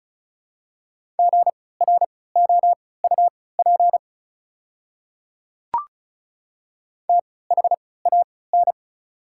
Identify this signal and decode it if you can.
Morse code, heard as a keyed tone, about 36 wpm, spaced as for 20 wpm, 690 Hz. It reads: GROUP THAN
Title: 3x Character Spacing - Top 500 Words - 35wpm